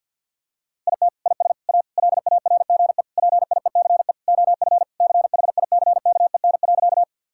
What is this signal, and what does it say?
Morse code, heard as a keyed tone, about 50 wpm, 700 Hz. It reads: IT IS A LARGE PIECE OF CHICKEN.
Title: Sentences from Top 1000 Words - Review Part 5 - 50wpm